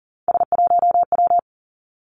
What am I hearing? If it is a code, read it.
Morse code, heard as a keyed tone, about 42 wpm, 700 Hz. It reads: S1W
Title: Three Challenging Letter-Number Combinations 40wpm